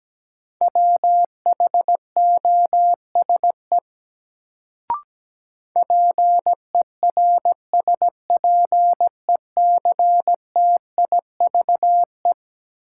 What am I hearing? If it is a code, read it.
Morse code, heard as a keyed tone, about 17 wpm, 700 Hz. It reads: WHOSE PERSPECTIVE
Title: Sentences from Top 1200 Words - New Word Drill - 17wpm